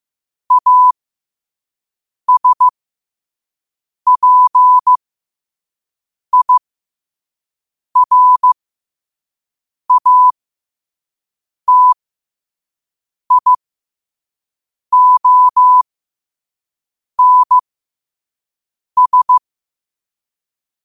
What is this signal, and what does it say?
Morse code, heard as a keyed tone, about 15 wpm, spaced as for 5 wpm, 1000 Hz. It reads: ASPIRATIONS
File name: qotdfiles / 5wpm